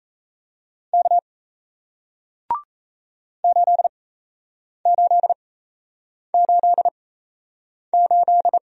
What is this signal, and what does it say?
Morse code, heard as a keyed tone, about 38 wpm, spaced as for 8 wpm, 700 Hz. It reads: K 8888